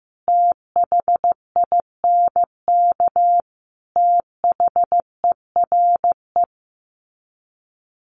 Morse code, heard as a keyed tone, about 15 wpm, 700 Hz. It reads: THINK THERE
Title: Sets of 5 Words 15wpm